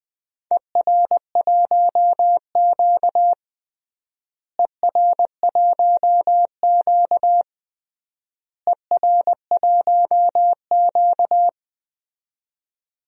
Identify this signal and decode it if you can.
Morse code, heard as a keyed tone, about 21 wpm, 700 Hz. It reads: ER1Q ER1Q ER1Q
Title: DXCC-100 Call Signs - 4 Characters - Repeated 3x - 20wpm